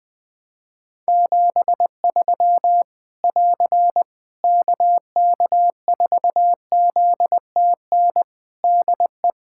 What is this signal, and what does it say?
Morse code, heard as a keyed tone, about 21 wpm, 700 Hz. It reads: 73 <AR> KK4ZTN DE
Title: Practice QSO - Exchange 4 - Ending - 20wpm